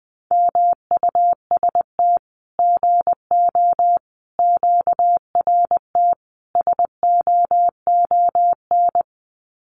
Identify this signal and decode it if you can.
Morse code, heard as a keyed tone, about 20 wpm, 700 Hz. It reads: MUST GO QRT SOON
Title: Common QSO Phrases 20wpm